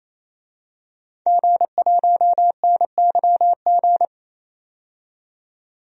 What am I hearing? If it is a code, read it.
Morse code, heard as a keyed tone, about 28 wpm, 700 Hz. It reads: G1NYG
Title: DXCC-100 Call Signs - 5 Characters - 28wpm